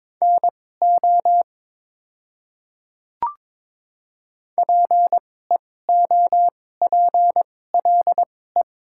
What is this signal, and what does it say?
Morse code, heard as a keyed tone, about 22 wpm, spaced as for 16 wpm, 700 Hz. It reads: NO PEOPLE
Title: 2x Character Spacing - Top 100 Words - 22wpm